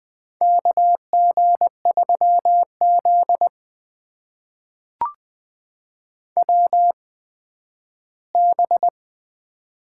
Morse code, heard as a keyed tone, about 20 wpm, spaced as for 12 wpm, 700 Hz. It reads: KG3Z WB
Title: USA Call Signs - 4 Characters - 8x Character Spacing - 20wpm